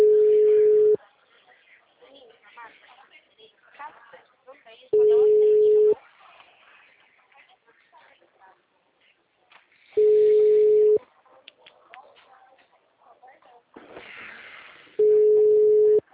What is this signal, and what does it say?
Ringback tone (Europe, 425 Hz), with speech.
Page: Limeira - Telemarket